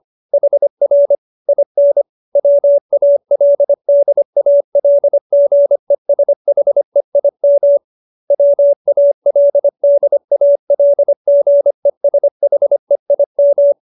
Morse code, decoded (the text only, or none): HR IN WALDALGESHEIM WALDALGESHEIM